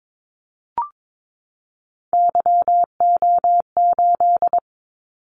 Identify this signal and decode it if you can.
Morse code, heard as a keyed tone, about 22 wpm, 700 Hz. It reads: YO8
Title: Callsign Prefixes for Top 25 DXCC - 22wpm